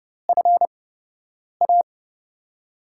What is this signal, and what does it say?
Morse code, heard as a keyed tone, about 30 wpm, spaced as for 8 wpm, 700 Hz. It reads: FA
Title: USA Call Signs - 8x Character Spacing - 30wpm